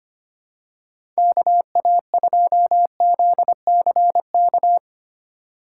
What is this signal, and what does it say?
Morse code, heard as a keyed tone, about 25 wpm, 700 Hz. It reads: KA2ZCK
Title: USA Call Signs - 6 Characters - 2x Character Spacing - 25wpm